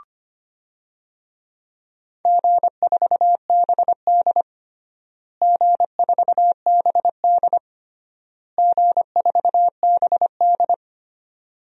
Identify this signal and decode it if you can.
Morse code, heard as a keyed tone, about 25 wpm, 700 Hz. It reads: G4BD G4BD G4BD
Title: DXCC-100 Call Signs - 4 Characters - Repeated 3x - 25wpm